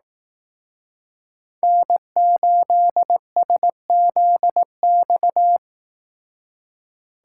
Morse code, heard as a keyed tone, about 18 wpm, 700 Hz. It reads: N8SZX
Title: Speed Racing - USA 1x3 Call Signs 15wpm